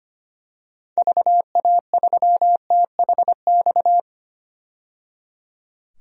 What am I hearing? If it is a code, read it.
Morse code, heard as a keyed tone, about 25 wpm, 690 Hz. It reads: VA3THX
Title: DXCC-100 Call Signs - 6 Characters - 25wpm